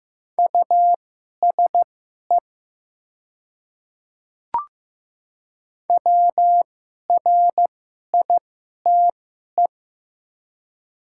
Morse code, heard as a keyed tone, about 15 wpm, spaced as for 11 wpm, 700 Hz. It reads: USE WRITE
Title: 2x Character Spacing - Top 100 Words - 15wpm